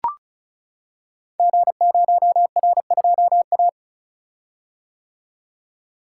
Morse code, heard as a keyed tone, about 36 wpm, 690 Hz. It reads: G0R2A
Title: Code Groups - 5 Letter:Numbers - 35wpm